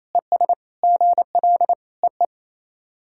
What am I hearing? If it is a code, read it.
Morse code, heard as a keyed tone, about 28 wpm, 700 Hz. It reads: ES GL EE
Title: SOTA QSO Practice 28wpm